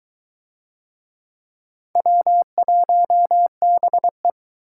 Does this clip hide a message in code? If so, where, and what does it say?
Yes: Morse code from 2.0 to 4.3 s, heard as a keyed tone, about 22 wpm, 700 Hz. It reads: W1BE